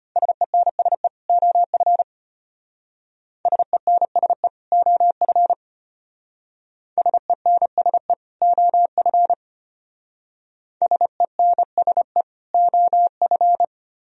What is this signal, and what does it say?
Morse code, heard as a keyed tone, about 32 wpm, 700 Hz. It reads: SENSE OF SENSE OF SENSE OF SENSE OF